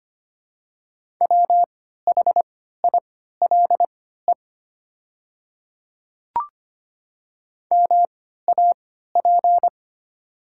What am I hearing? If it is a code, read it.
Morse code, heard as a keyed tone, about 25 wpm, spaced as for 14 wpm, 700 Hz. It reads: WHILE MAP
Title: 3x Character Spacing - Top 500 Words - 25wpm